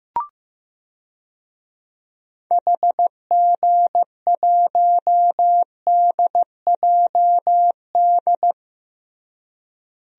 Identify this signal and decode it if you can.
Morse code, heard as a keyed tone, about 15 wpm, 700 Hz. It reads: HG1DJD
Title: DXCC-100 Call Signs - 6 Characters - Repeated 3x - 15wpm